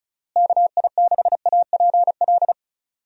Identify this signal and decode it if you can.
Morse code, heard as a keyed tone, about 36 wpm, 700 Hz. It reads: KI6APL
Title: USA Call Signs - 6 Characters - Repeated 3x - 35wpm